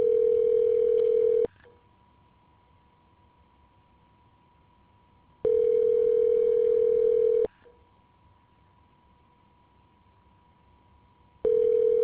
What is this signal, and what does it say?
Ringback tone (North America), 440 + 480 Hz.